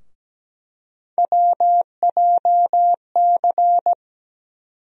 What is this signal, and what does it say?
Morse code, heard as a keyed tone, about 17 wpm, 700 Hz. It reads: WJC
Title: USA Call Sign Postfixes - 3 Letters 17wpm